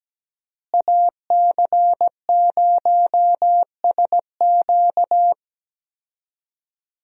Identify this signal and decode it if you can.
Morse code, heard as a keyed tone, about 17 wpm, 700 Hz. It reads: AC0SQ
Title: USA Call Signs - 5 Characters - Repeated 3x - 17wpm